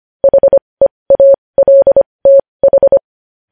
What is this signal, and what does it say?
Morse code, heard as a keyed tone, about 25 wpm, spaced as for 20 wpm, 550 Hz. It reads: HEALTH
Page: news-headlines-in-morse-code-at-20-wpm